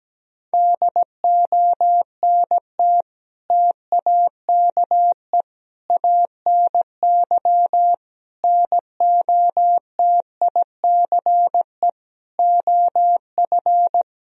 Morse code, heard as a keyed tone, about 17 wpm, 700 Hz. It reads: DONT TAKE ANY NOTICE OF
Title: Sentences from Top 1000 Words - Review Part 2 - 17wpm